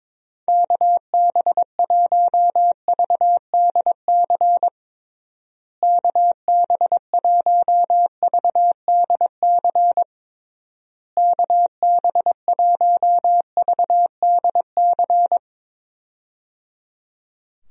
Morse code, heard as a keyed tone, about 21 wpm, 700 Hz. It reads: KB1VDC KB1VDC KB1VDC
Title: USA Call Signs - 6 Characters - Repeated 3x - 22wpm